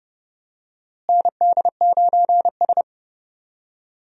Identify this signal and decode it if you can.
Morse code, heard as a keyed tone, about 30 wpm, 700 Hz. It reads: ND9S
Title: USA Call Signs - 4 Characters - Repeated 3x - 30wpm